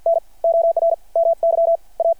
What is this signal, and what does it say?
Morse code, heard as a keyed tone, about 53 wpm, spaced as for 38 wpm, 600 Hz. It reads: NOR MY A
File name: archived / CRSnet